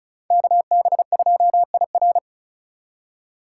Morse code, heard as a keyed tone, about 45 wpm, 700 Hz. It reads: KB2IR